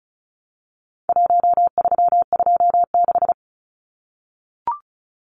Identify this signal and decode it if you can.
Morse code, heard as a keyed tone, about 34 wpm, 700 Hz. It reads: 1326
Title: Numbers - 4-Digit 35wpm